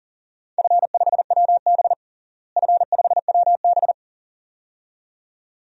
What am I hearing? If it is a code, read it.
Morse code, heard as a keyed tone, about 37 wpm, 700 Hz. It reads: F5WB F5WB
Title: DXCC-100 Call Signs - 4 Characters - Repeated 2x - 40wpm